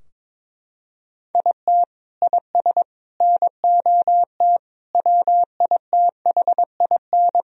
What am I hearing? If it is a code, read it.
Morse code, heard as a keyed tone, about 22 wpm, 700 Hz. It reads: IT IS NOT WITHIN
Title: Sentences from Top 1000 Words - Review Part 4 - 22wpm